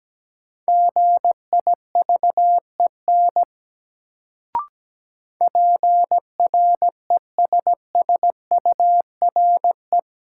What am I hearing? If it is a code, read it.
Morse code, heard as a keyed tone, about 17 wpm, 700 Hz. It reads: GIVEN PRESSURE
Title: Sentences from Top 1400 Words - New Word Drill Alt - 17wpm